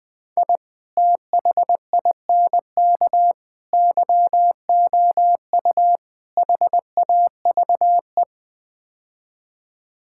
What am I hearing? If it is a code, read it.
Morse code, heard as a keyed tone, about 20 wpm, 700 Hz. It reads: I THINK YOU HAVE
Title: Build A Phrase - 5-Word Phrases - 20wpm